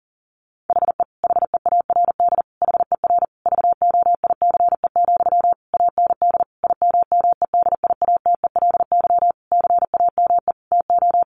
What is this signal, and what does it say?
Morse code, heard as a keyed tone, about 40 wpm, 700 Hz. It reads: HE HEARD HER VOICE, AND IMMEDIATELY CAME TO
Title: Sentences from Top 1100 Words 40wpm